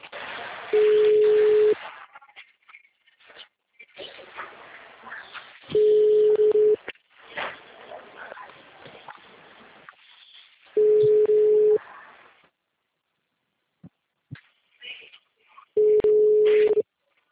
Ringback tone (Europe, 425 Hz).